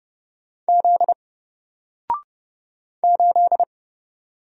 Morse code, heard as a keyed tone, about 30 wpm, 700 Hz. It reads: Z 8
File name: Single-Letter-Number-30wpm